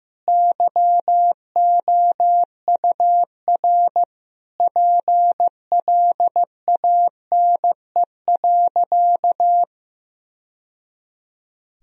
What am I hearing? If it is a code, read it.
Morse code, heard as a keyed tone, about 16 wpm, 700 Hz. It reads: YOUR PLANE.
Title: Sentences from Top 1000 Words - Review Part 3 - 15wpm